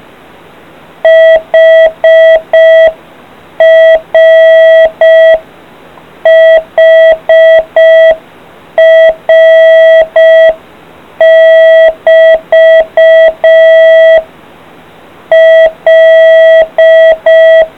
Morse code, heard as a keyed tone, about 5 wpm, 650 Hz. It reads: HR HR <BT> L